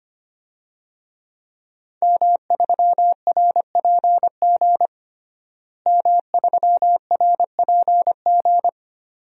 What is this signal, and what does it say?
Morse code, heard as a keyed tone, about 25 wpm, 700 Hz. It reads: M3RPG M3RPG
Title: DXCC-100 Call Signs - 5 Characters - Repeated 2x - 25wpm